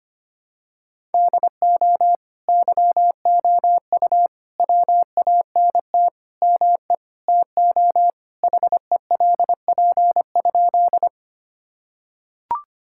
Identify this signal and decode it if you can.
Morse code, heard as a keyed tone, about 25 wpm, 700 Hz. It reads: DO YOU WANT ME TO HELP?
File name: practice / 25wpm / Sentences with 4-Word Phrases - 25wpm